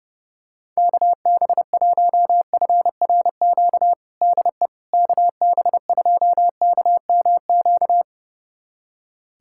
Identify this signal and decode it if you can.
Morse code, heard as a keyed tone, about 30 wpm, 700 Hz. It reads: KB1FRQ DE KB2KMQ
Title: USA Callsign DE Callsign - 6 Characters - 30wpm